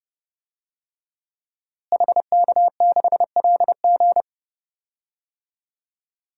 Morse code, heard as a keyed tone, about 31 wpm, 700 Hz. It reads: HK6LG
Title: DXCC-100 Call Signs - 5 Characters - Repeated 3x - 30wpm